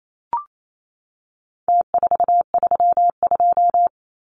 Morse code, heard as a keyed tone, about 28 wpm, 700 Hz. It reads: T432
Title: Numbers - Contest Style - 4-Digit 28wpm